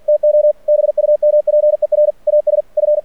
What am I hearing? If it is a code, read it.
Morse code, heard as a keyed tone, about 49 wpm, 600 Hz. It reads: TO BUMPER AS F
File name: archived / CRSnet